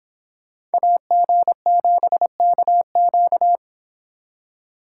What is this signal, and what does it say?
Morse code, heard as a keyed tone, about 30 wpm, 700 Hz. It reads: AG7KQ